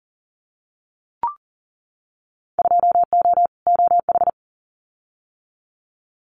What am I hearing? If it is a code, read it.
Morse code, heard as a keyed tone, about 41 wpm, 700 Hz. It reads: 2O OH
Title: Winter Field Day Exchanges 40wpm